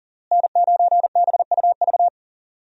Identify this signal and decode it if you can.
Morse code, heard as a keyed tone, about 40 wpm, 700 Hz. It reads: N9BUV
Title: USA Call Signs - 5 Characters - Repeated 3x - 40wpm